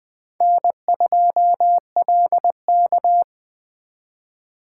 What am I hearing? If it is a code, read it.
Morse code, heard as a keyed tone, about 20 wpm, 700 Hz. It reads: N2LK
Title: K1USN SST QSO Practice - 20wpm